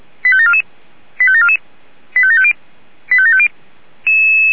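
Five-tone selective call (ZVEI), heard as quick runs of five tones, tones of about 70 ms. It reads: 85744 85744 85750 85750
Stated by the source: ZVEI five-tone call received by SDR.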